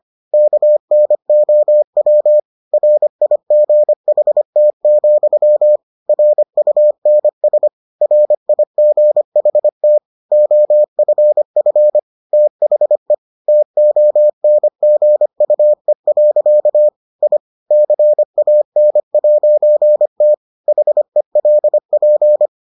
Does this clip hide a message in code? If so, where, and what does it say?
Yes: Morse code from 0.3 to 22.5 s, heard as a keyed tone, about 25 wpm, 600 Hz. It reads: KNOW RIGHT, RUNS RIGHT OFF THE TONGUE. I CAN'T HELP